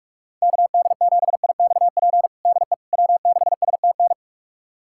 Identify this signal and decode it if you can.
Morse code, heard as a keyed tone, about 42 wpm, 700 Hz. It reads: KD7IXP DE W6STN
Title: USA Callsign DE Callsign 45wpm